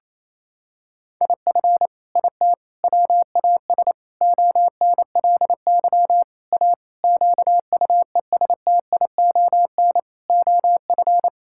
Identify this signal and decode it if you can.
Morse code, heard as a keyed tone, about 28 wpm, 700 Hz. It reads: IF IT WAS ONLY A QUESTION OF
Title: Sentences from Top 1600 Words 28wpm